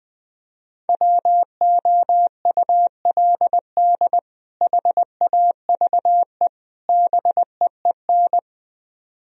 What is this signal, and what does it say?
Morse code, heard as a keyed tone, about 21 wpm, 700 Hz. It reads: WOULD HAVE BEEN